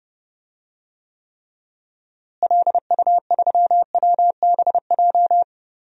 Morse code, heard as a keyed tone, about 30 wpm, 700 Hz. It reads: LU3WBJ
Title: DXCC-100 Call Signs - 6 Characters - Repeated 3x - 30wpm